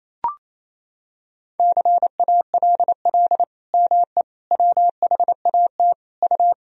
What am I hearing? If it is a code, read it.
Morse code, heard as a keyed tone, about 28 wpm, 700 Hz. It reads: CALL ME WHAT U
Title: Sentences from Top 200 Words and Abbreviations 28wpm